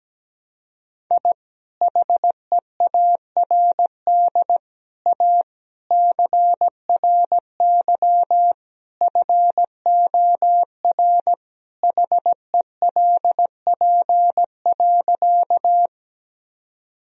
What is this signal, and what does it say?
Morse code, heard as a keyed tone, about 17 wpm, 700 Hz. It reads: I HEARD A CRY FOR HELP.
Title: Sentences from Top 1000 Words - Review Part 2 - 17wpm